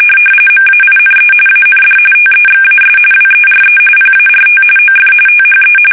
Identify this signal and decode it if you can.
RTTY, about 45.45 baud. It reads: MAINE AND TEXAS WERE PART OF THE
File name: fsk850 / battleships